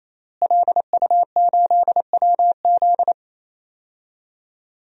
Morse code, heard as a keyed tone, about 28 wpm, 700 Hz. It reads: LU8WZ
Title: DXCC-100 Call Signs - 5 Characters - Repeated 3x - 28wpm